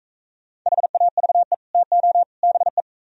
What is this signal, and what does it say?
Morse code, heard as a keyed tone, about 40 wpm, 700 Hz. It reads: HAVE TO BE